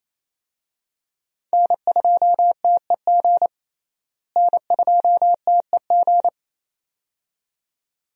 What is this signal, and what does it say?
Morse code, heard as a keyed tone, about 27 wpm, 700 Hz. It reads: N2TEG N2TEG
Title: USA Call Signs - 5 Characters - Repeated 2x - 28wpm